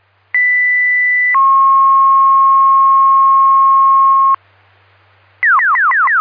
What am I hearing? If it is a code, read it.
Two-tone page, heard as a one-second tone, then a three-second tone, about 2000 then 1080 Hz. It is heard from 0.3 to 4.4 s.
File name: Newton EMS Duty